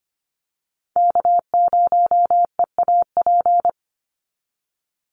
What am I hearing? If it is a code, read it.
Morse code, heard as a keyed tone, about 25 wpm, 700 Hz. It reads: K0EAP
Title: NA SOTA Chaser Callsigns 25wpm